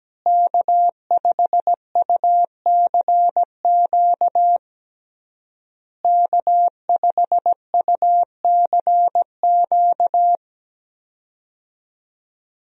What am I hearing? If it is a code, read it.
Morse code, heard as a keyed tone, about 17 wpm, 700 Hz. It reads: K5UCQ K5UCQ
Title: USA Call Signs - 5 Characters - Repeated 3x - 17wpm